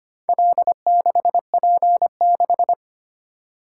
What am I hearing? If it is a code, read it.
Morse code, heard as a keyed tone, about 25 wpm, 700 Hz. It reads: L6P6
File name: practice / 25wpm / Four Letter-Number Combinations - 25wpm